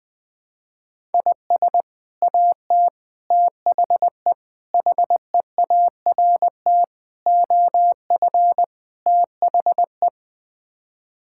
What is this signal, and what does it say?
Morse code, heard as a keyed tone, about 21 wpm, 700 Hz. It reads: IS AT THE HEART OF THE
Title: Sets of 6 Words - 20wpm